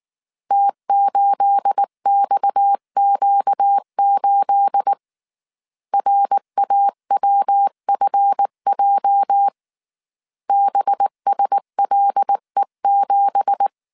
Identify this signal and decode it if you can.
Morse code, heard as a keyed tone, about 19 wpm, 800 Hz. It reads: T8XQ8 RAWFJ BSLE7
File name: CW / DET / 1 9 1500C S C 18wpm 2539